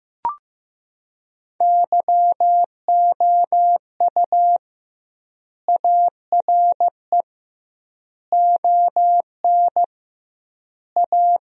Morse code, heard as a keyed tone, about 15 wpm, 700 Hz. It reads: YOU ARE ON A
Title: 2x Word-Spacing - Sentences from Top 400 Words - 15wpm